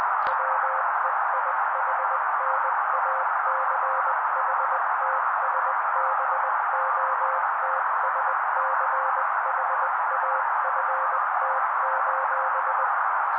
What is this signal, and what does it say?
Morse code, heard as a keyed tone, about 20 wpm, 530 Hz. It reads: WEIHNACHTSBOTSCHAFT:
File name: daily / 18wpm noise